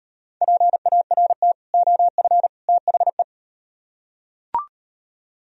Morse code, heard as a keyed tone, about 38 wpm, 700 Hz. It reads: PART OF THE